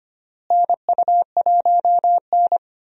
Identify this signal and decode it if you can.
Morse code, heard as a keyed tone, about 25 wpm, 700 Hz. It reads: NU1N
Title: USA Call Signs - 4 Characters - Repeated 3x - 25wpm